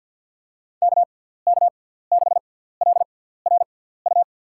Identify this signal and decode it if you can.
Morse code, heard as a keyed tone, about 53 wpm, spaced as for 17 wpm, 700 Hz. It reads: KK6LRU
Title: USA Call Signs - 6 Characters - 6x Character Spacing - 50wpm